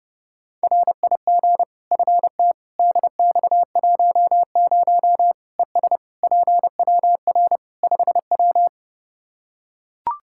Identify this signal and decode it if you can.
Morse code, heard as a keyed tone, about 30 wpm, 700 Hz. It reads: RIG FT DX10 ES PWR 5W
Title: Practice QSO - Exchange 2 - Info - 30wpm